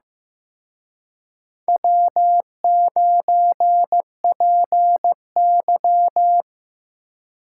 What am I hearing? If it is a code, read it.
Morse code, heard as a keyed tone, about 18 wpm, 700 Hz. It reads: W9PY
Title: Speed Racing - USA 1x2 Call Signs 15wpm